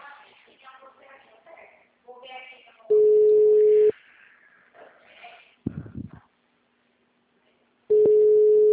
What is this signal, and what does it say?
Ringback tone (Europe, 425 Hz), with speech.
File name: out- -2005- - - .30268